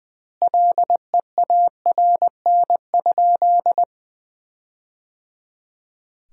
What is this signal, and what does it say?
Morse code, heard as a keyed tone, about 20 wpm, 700 Hz. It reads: LEARN?
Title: 2x Word-Spacing - Sentences from Top 300 Words - 20wpm